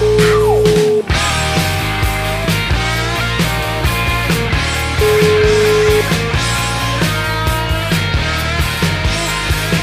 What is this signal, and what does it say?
Ringback tone (Europe, 425 Hz), with noise.